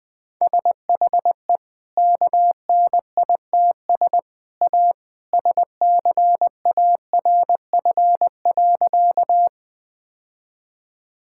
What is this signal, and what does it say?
Morse code, heard as a keyed tone, about 21 wpm, 700 Hz. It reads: SHE KNITS A SCARF.
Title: Easy Sentences with Hints - 20wpm